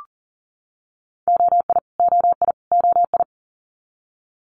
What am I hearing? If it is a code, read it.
Morse code, heard as a keyed tone, about 40 wpm, 700 Hz. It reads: OI OI OI
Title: Multi-letter Phonemes Repeated 3x 40wpm